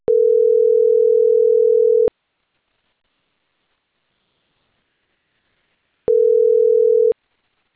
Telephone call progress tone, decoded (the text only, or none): ringback tone